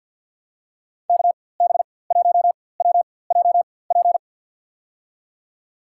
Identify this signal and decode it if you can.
Morse code, heard as a keyed tone, about 53 wpm, spaced as for 24 wpm, 700 Hz. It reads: KB1WJP
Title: USA Call Signs - 6 Characters - 4x Character Spacing - 50wpm